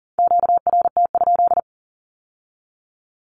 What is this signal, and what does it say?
Morse code, heard as a keyed tone, about 39 wpm, 700 Hz. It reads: QRT?
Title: Q-Codes with Spoken Meaning - 40wpm